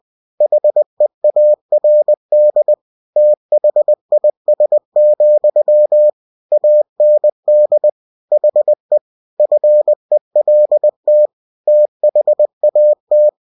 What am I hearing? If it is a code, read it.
Morse code, heard as a keyed tone, about 20 wpm, 600 Hz. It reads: HEARD THIS, AND HE FELT THAT
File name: cw-books / winnie-the-pooh / winnie-20wpm-page-0035